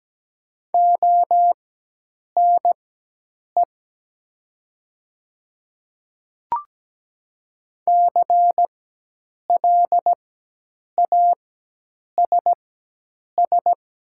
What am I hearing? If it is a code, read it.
Morse code, heard as a keyed tone, about 17 wpm, spaced as for 8 wpm, 700 Hz. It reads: ONE CLASS
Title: 4x Character Spacing - Top 500 Words - 17wpm